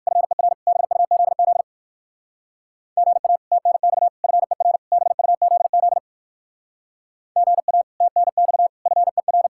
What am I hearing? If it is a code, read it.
Morse code, heard as a keyed tone, about 57 wpm, 700 Hz. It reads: FER BUZZ GA TNX FER BUZZ GA TNX FER